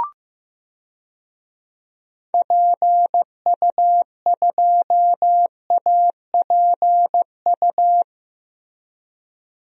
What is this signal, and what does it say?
Morse code, heard as a keyed tone, about 15 wpm, 700 Hz. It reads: PU2APU
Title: DXCC-100 Call Signs - 6 Characters - Repeated 2x - 15wpm